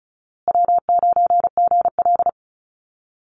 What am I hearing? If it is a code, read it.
Morse code, heard as a keyed tone, about 34 wpm, 700 Hz. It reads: W9GL
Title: Call Signs Repeated 2x - 35wpm